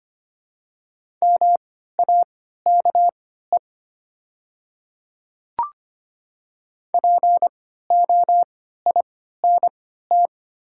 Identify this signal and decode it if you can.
Morse code, heard as a keyed tone, about 25 wpm, spaced as for 14 wpm, 690 Hz. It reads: MAKE POINT
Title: 3x Character Spacing - Top 200 Words - 25wpm